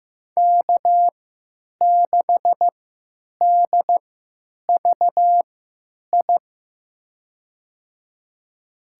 Morse code, heard as a keyed tone, about 15 wpm, spaced as for 9 wpm, 700 Hz. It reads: K6DVI
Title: USA Call Signs - 5 Characters - 3x Character Spacing - 15wpm